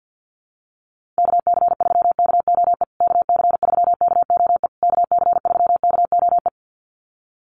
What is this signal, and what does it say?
Morse code, heard as a keyed tone, about 50 wpm, 700 Hz. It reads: KC3KOE KC3KOE KC3KOE
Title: USA Call Signs Repeated 3x 50wpm